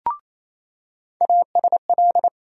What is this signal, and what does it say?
Morse code, heard as a keyed tone, about 28 wpm, 700 Hz. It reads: ASL
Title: USA Call Sign Postfixes - 3 Letters 28wpm